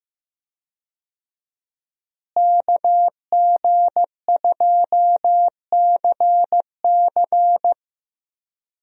Morse code, heard as a keyed tone, about 15 wpm, 700 Hz. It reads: KG2CC
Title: USA Call Signs - 5 Characters - Repeated 2x - 15wpm